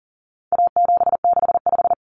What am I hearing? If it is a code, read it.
Morse code, heard as a keyed tone, about 40 wpm, 700 Hz. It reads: A765
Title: Common Cut Numbers - 4-Digit 40wpm